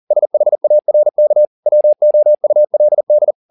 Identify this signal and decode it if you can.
Morse code, heard as a keyed tone, about 40 wpm, 600 Hz. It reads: SHARK WOULD
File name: the old man and the sea 2 40wpm